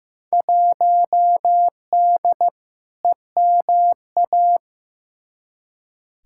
Morse code, heard as a keyed tone, about 15 wpm, 700 Hz. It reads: 1D EMA
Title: Field Day Exchanges - 15wpm